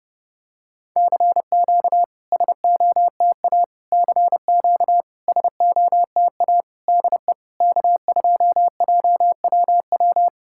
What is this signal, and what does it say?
Morse code, heard as a keyed tone, about 32 wpm, 690 Hz. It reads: CQ SOTA CQ SOTA DE K2JWW
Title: SOTA QSO Practice 30wpm